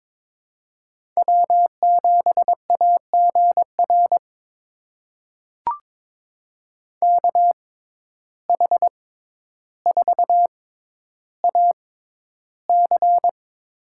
Morse code, heard as a keyed tone, about 21 wpm, spaced as for 11 wpm, 700 Hz. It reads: W7AGR KH4AC
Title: USA Call Signs - 5 Characters - 6x Character Spacing - 22wpm